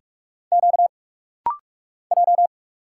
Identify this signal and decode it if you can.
Morse code, heard as a keyed tone, about 46 wpm, 700 Hz. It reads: Q J
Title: Single Letters - Rapid-Fire 45wpm